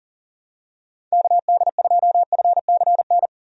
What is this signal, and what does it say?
Morse code, heard as a keyed tone, about 40 wpm, 700 Hz. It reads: KD2FCN
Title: USA Call Signs - 6 Characters - 40wpm